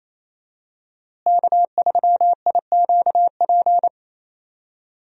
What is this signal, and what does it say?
Morse code, heard as a keyed tone, about 27 wpm, 700 Hz. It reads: K3IQP
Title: USA Call Signs - 5 Characters - Repeated 3x - 28wpm